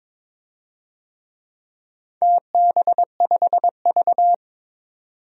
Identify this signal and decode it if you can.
Morse code, heard as a keyed tone, about 22 wpm, 700 Hz. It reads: TB5V
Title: DXCC-100 Call Signs - 4 Characters - 22wpm